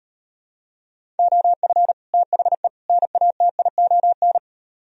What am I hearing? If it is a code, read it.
Morse code, heard as a keyed tone, about 38 wpm, 700 Hz. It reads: OF THE NATION